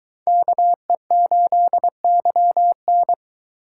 Morse code, heard as a keyed tone, about 22 wpm, 700 Hz. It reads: KE8YN